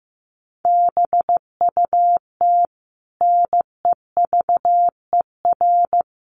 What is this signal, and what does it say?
Morse code, heard as a keyed tone, about 15 wpm, 700 Hz. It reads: BUT NEVER
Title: Sentences from Top 500 Words 15wpm